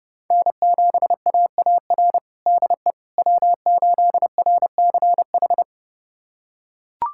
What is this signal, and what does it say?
Morse code, heard as a keyed tone, about 30 wpm, 700 Hz. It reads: N7AAR DE W8RCH
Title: USA Callsign DE Callsign - 5 Characters - 30wpm